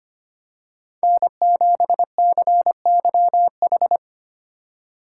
Morse code, heard as a keyed tone, about 25 wpm, 700 Hz. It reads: N7CYH
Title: USA Call Signs - 5 Characters - 8x Character Spacing - 25wpm